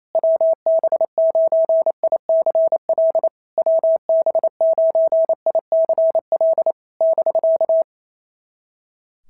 Morse code, heard as a keyed tone, about 28 wpm, 660 Hz. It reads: WB9ICL WB9ICL <BK>
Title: SOTA QSO Practice 28wpm